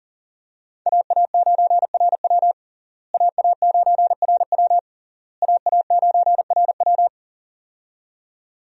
Morse code, heard as a keyed tone, about 40 wpm, 700 Hz. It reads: AA9RW AA9RW AA9RW
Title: USA Call Signs - 5 Characters - Repeated 3x - 40wpm